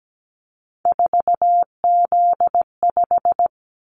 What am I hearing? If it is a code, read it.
Morse code, heard as a keyed tone, about 17 wpm, 700 Hz. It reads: 4Z5
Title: Callsign Prefixes for Top 50 DXCC - 17wpm